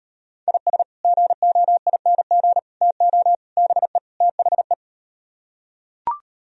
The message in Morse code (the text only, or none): IS GOING TO BE THE